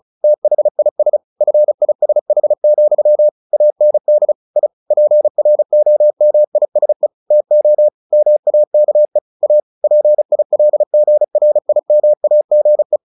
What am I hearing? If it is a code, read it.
Morse code, heard as a keyed tone, about 34 wpm, 600 Hz. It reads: THIS FISH, AND I PROMISE TO MAKE A PILGRIMAGE
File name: the old man and the sea / the old man and the sea 1 35wpm